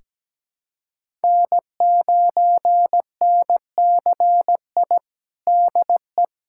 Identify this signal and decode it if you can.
Morse code, heard as a keyed tone, about 17 wpm, 700 Hz. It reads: N9NCI DE
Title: USA Callsign DE Callsign - 5 Characters - 17wpm